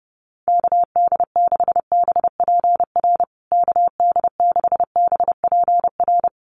Morse code, heard as a keyed tone, about 30 wpm, 700 Hz. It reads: KD6BPR KD6BPR
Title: USA Call Signs Repeated 2x 30wpm